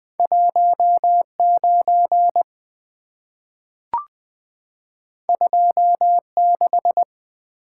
Morse code, heard as a keyed tone, about 20 wpm, 690 Hz. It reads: 19 26
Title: Two Letter-Number Combinations - 20wpm